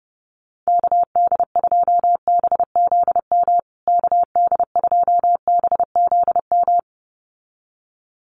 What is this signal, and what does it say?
Morse code, heard as a keyed tone, about 30 wpm, 700 Hz. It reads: KD2BZM KD2BZM
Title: USA Call Signs Repeated 2x 30wpm